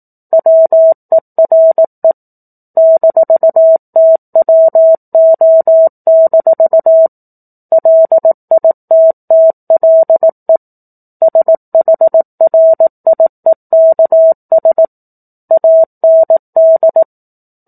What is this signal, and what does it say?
Morse code, heard as a keyed tone, about 18 wpm, 650 Hz. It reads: WERE -TWO- LITTLE SHRIEKS AND
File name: 04-Chapter-04 18 WPM